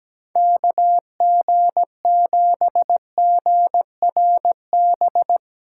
Morse code, heard as a keyed tone, about 17 wpm, 700 Hz. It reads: KG7GRB
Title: USA Callsign DE Callsign - 6 Characters - 17wpm